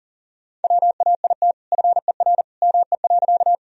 Morse code, heard as a keyed tone, about 40 wpm, 700 Hz. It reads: WAIT FER ME.
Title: Sentences from Top 500 Words and Abbreviations 40wpm